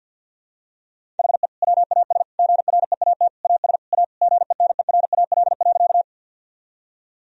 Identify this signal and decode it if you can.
Morse code, heard as a keyed tone, about 52 wpm, 700 Hz. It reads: HE WAS GREAT AS A GENERAL.
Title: Sentences from Top 1000 Words - Review Part 3 - 50wpm